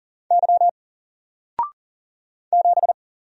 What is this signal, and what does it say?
Morse code, heard as a keyed tone, about 40 wpm, 700 Hz. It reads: Y 7
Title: Single Letter-Number Alt 40wpm